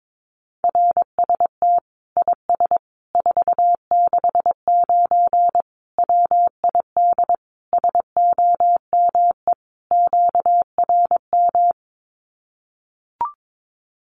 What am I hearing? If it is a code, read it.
Morse code, heard as a keyed tone, about 22 wpm, 700 Hz. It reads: RST IS 469 WID SOME QRM
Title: Common QSO Phrases 22wpm